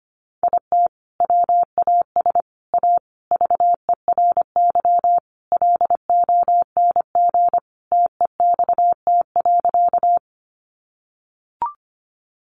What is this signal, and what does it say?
Morse code, heard as a keyed tone, about 25 wpm, 700 Hz. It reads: IT WAS A VERY LONG TEXT.
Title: Sentences from Top 1200 Words 25wpm